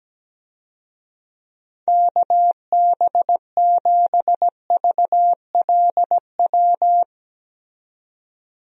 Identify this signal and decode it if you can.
Morse code, heard as a keyed tone, about 17 wpm, 700 Hz. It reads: KB7VLW
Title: USA Call Signs - 6 Characters - 17wpm